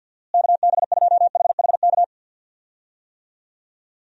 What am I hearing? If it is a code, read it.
Morse code, heard as a keyed tone, about 47 wpm, 700 Hz. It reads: KB2HHK
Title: USA Call Signs - 6 Characters - Repeated 2x - 50wpm